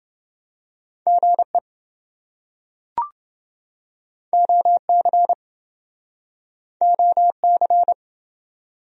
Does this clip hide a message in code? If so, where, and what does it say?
Yes: Morse code from 1.1 to 7.9 s, heard as a keyed tone, about 27 wpm, 700 Hz. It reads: GE OC OC